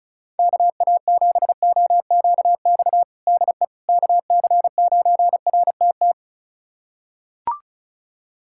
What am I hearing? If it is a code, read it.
Morse code, heard as a keyed tone, about 36 wpm, 690 Hz. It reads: KA7OQX DE KC9RTT
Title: USA Callsign DE Callsign - 6 Characters - 35wpm